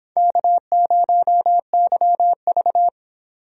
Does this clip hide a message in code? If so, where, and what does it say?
Yes: Morse code from 0.2 to 2.9 s, heard as a keyed tone, about 25 wpm, 700 Hz. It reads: K0YV